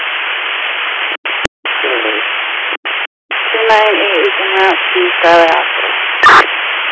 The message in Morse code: none

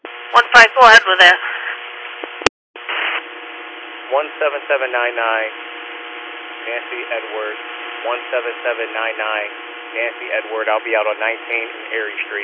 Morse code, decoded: none

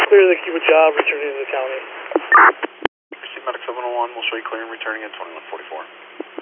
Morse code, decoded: none

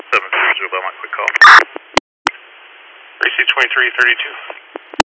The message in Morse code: none

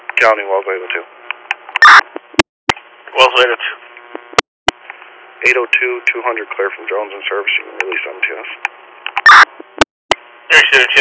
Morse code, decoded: none

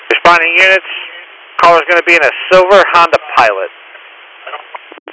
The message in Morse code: none

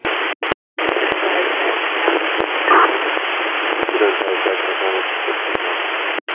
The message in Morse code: none

